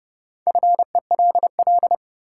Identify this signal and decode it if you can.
Morse code, heard as a keyed tone, about 31 wpm, 700 Hz. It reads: FELL
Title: Sentences from Top 1600 Words - New Word Drill Alt - 30wpm